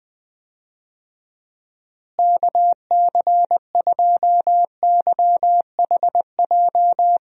Morse code, heard as a keyed tone, about 20 wpm, 700 Hz. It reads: KC2YHJ
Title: USA Call Signs - 6 Characters - Repeated 3x - 20wpm